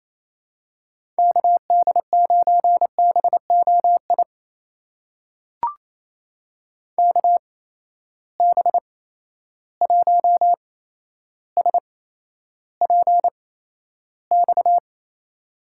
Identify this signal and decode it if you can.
Morse code, heard as a keyed tone, about 28 wpm, spaced as for 12 wpm, 700 Hz. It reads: KD9BOI KB1SPX
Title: USA Call Signs - 6 Characters - 8x Character Spacing - 28wpm